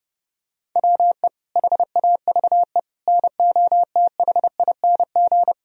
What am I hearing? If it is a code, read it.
Morse code, heard as a keyed tone, about 30 wpm, 700 Hz. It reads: WE HAVE NOTHING
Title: Sentences from Top 1000 Words - Review Part 1 - 30wpm